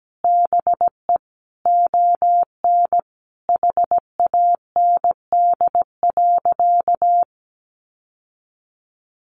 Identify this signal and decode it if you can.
Morse code, heard as a keyed tone, about 17 wpm, 700 Hz. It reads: BE ON HAND.
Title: Sentences from Top 1500 Words 17wpm